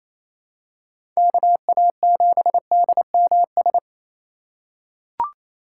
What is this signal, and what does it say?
Morse code, heard as a keyed tone, about 28 wpm, 700 Hz. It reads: KA7DMS
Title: USA Call Signs - 6 Characters - 2x Character Spacing - 28wpm